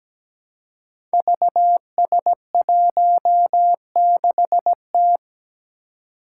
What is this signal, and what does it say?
Morse code, heard as a keyed tone, about 17 wpm, 700 Hz. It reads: VS16T
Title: Five Challenging Letter-Number Combinations 17wpm